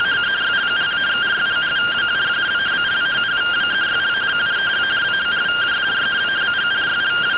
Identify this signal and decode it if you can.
Two-tone teleprinter signal, about 170 Hz shift.